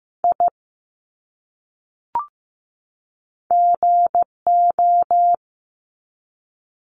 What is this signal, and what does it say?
Morse code, heard as a keyed tone, about 15 wpm, 700 Hz. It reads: I GO
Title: Top 300 Words 15wpm